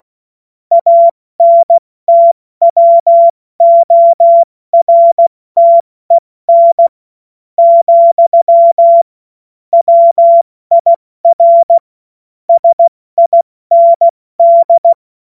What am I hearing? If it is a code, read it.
Morse code, heard as a keyed tone, about 16 wpm, spaced as for 13 wpm, 680 Hz. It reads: ANTWORTEN, WIR SIND
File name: Morsekurs / Witze / Witz 17. Januar 2023 70 F80